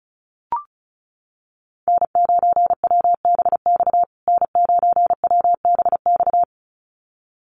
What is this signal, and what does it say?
Morse code, heard as a keyed tone, about 34 wpm, 700 Hz. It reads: N9WBX N9WBX
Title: USA Call Signs Repeated 2x 35wpm